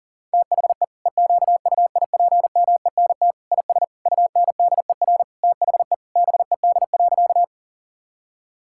Morse code, heard as a keyed tone, about 37 wpm, 700 Hz. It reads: THE EQUIPMENT IS UNDER THE BED.
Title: Sentences from Top 1000 Words - Review Part 5 - 40wpm